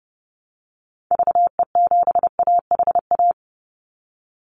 Morse code, heard as a keyed tone, about 30 wpm, 700 Hz. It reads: VE7AHA
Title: Call Signs Repeated 2x - 30wpm